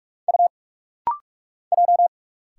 Morse code, heard as a keyed tone, about 46 wpm, 700 Hz. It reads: U J
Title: Single Letters - Rapid-Fire 45wpm